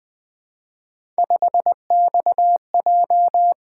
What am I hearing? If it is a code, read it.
Morse code, heard as a keyed tone, about 20 wpm, 700 Hz. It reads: 5XJ
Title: Three Letter-Number Combinations - 20wpm